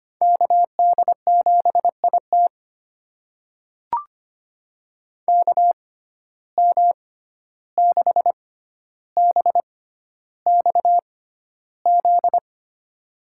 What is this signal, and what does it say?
Morse code, heard as a keyed tone, about 25 wpm, spaced as for 12 wpm, 700 Hz. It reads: KD7IT KM6BXZ
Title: USA Call Signs - 6x Character Spacing - 25wpm